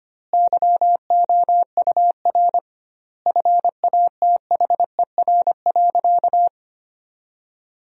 Morse code, heard as a keyed tone, about 25 wpm, 700 Hz. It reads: YOUR FATHER.
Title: 2x Word-Spacing - Sentences from Top 300 Words - 25wpm